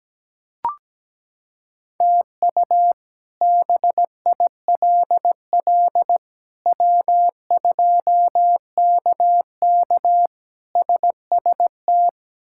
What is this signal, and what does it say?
Morse code, heard as a keyed tone, about 17 wpm, 700 Hz. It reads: TU BILL W2KK SST
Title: K1USN SST QSO Practice - 17wpm